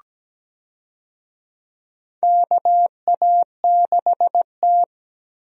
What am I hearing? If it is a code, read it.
Morse code, heard as a keyed tone, about 17 wpm, 700 Hz. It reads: KA6T
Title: USA Call Signs - 4 Characters - Repeated 2x - 17wpm